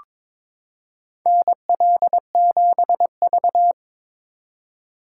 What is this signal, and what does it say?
Morse code, heard as a keyed tone, about 22 wpm, 700 Hz. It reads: NL7V
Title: K1USN SST QSO Practice - 22wpm